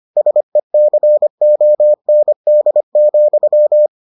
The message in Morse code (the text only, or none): SECOND,